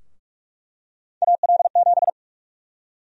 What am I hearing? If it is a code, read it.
Morse code, heard as a keyed tone, about 42 wpm, 700 Hz. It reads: AL7
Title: USA Call Sign Prefixes - Repeated 2x - 45wpm